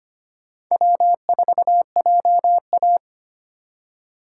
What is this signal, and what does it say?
Morse code, heard as a keyed tone, about 25 wpm, 700 Hz. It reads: W4JA
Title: USA Call Signs - 4 Characters - 4x Character Spacing - 25wpm